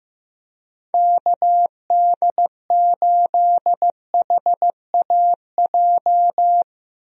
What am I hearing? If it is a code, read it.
Morse code, heard as a keyed tone, about 15 wpm, 700 Hz. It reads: KD8HAJ
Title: USA Call Signs - 6 Characters - Repeated 3x - 15wpm